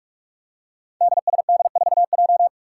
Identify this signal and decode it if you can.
Morse code, heard as a keyed tone, about 47 wpm, 700 Hz. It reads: DSD4J
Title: Five Challenging Letter-Number Combinations 45wpm